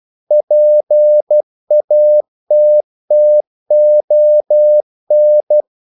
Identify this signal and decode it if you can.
Morse code, heard as a keyed tone, about 12 wpm, 600 Hz. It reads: PATTON
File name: CW / Practice / EnglishQuotations0112